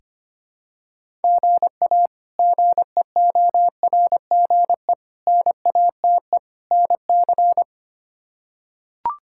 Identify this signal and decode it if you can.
Morse code, heard as a keyed tone, about 25 wpm, 700 Hz. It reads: GA GEORGE NATE NC
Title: K1USN SST QSO Practice - 25wpm